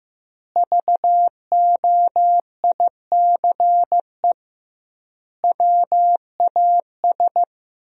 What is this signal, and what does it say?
Morse code, heard as a keyed tone, about 15 wpm, 700 Hz. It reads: VOICE WAS
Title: 2x Word-Spacing - Sentences from Top 500 Words - 15wpm